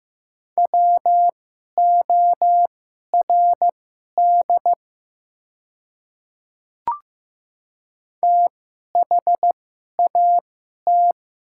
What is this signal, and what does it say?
Morse code, heard as a keyed tone, about 15 wpm, spaced as for 11 wpm, 690 Hz. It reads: WORD THAT
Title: 2x Character Spacing - Top 100 Words - 15wpm